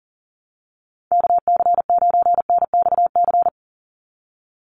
Morse code, heard as a keyed tone, about 39 wpm, 700 Hz. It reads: KC9NXC
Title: USA Call Signs 40wpm